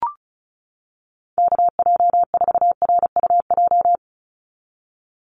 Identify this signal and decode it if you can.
Morse code, heard as a keyed tone, about 34 wpm, 700 Hz. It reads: KJ4RUJ
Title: USA Call Signs 35wpm